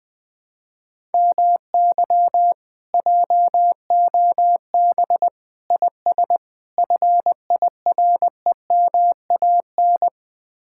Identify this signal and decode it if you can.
Morse code, heard as a keyed tone, about 20 wpm, 700 Hz. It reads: MY JOB IS FIREMAN
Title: Speed-Racing Common QSO Phrases - 17wpm